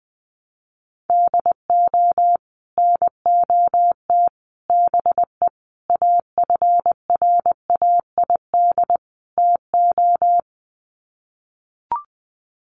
Morse code, heard as a keyed tone, about 19 wpm, 700 Hz. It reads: DO NOT BE AFRAID TO
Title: Sets of 5 Words Encore2 20wpm